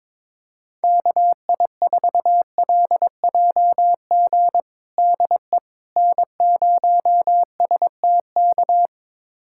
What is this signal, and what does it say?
Morse code, heard as a keyed tone, about 21 wpm, 700 Hz. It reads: KI4LJG DE N0STK
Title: USA Callsign DE Callsign 22wpm